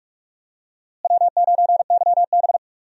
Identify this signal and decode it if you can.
Morse code, heard as a keyed tone, about 46 wpm, 700 Hz. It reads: W9YB
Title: USA Call Signs - 4 Characters - Repeated 2x - 45wpm